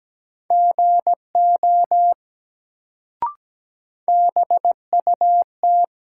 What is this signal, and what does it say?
Morse code, heard as a keyed tone, about 17 wpm, 700 Hz. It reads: GO BUT
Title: Sentences from Top 100 Words - New Word Drill - 17wpm